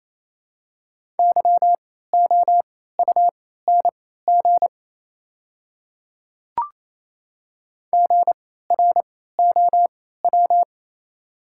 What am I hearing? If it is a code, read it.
Morse code, heard as a keyed tone, about 27 wpm, spaced as for 16 wpm, 690 Hz. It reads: YOUNG GROW
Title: 3x Character Spacing - Top 500 Words - 28wpm